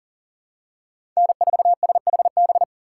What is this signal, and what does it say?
Morse code, heard as a keyed tone, about 37 wpm, 700 Hz. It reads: N4SHB
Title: USA Call Signs - 5 Characters - 4x Character Spacing - 40wpm